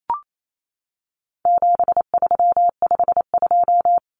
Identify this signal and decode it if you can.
Morse code, heard as a keyed tone, about 28 wpm, 700 Hz. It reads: 7352
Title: Numbers - 4-Digit 28wpm